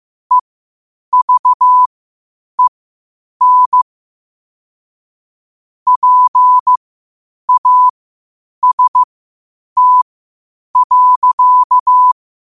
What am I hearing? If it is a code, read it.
Morse code, heard as a keyed tone, about 15 wpm, spaced as for 8 wpm, 1000 Hz. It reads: EVEN PAST.
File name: qotdfiles / 8wpm